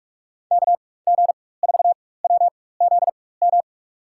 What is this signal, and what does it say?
Morse code, heard as a keyed tone, about 43 wpm, spaced as for 21 wpm, 700 Hz. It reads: KG4WZM
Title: USA Call Signs - 6 Characters - 4x Character Spacing - 45wpm